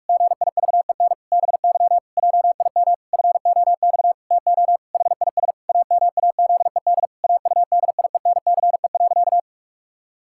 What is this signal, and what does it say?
Morse code, heard as a keyed tone, about 46 wpm, 700 Hz. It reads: GIVEN BY JIM FOX TO HIS AMAZED AUDIENCE.
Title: Pangram Challenge v2 - 45wpm